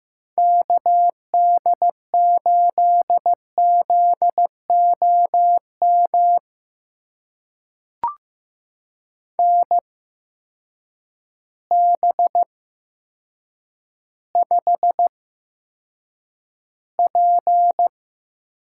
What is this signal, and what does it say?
Morse code, heard as a keyed tone, about 15 wpm, spaced as for 8 wpm, 700 Hz. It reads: KD8ZOM NB5P
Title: USA Call Signs - 8x Character Spacing - 15wpm